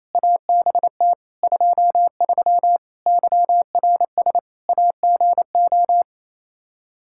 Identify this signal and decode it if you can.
Morse code, heard as a keyed tone, about 28 wpm, 700 Hz. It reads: ABT 23 YRS AGO
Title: Practice QSO - Exchange 3 - More Info - 28wpm